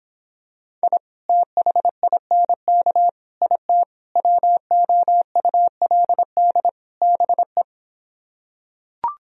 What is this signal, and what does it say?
Morse code, heard as a keyed tone, about 26 wpm, 700 Hz. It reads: I THINK IT WOULD BE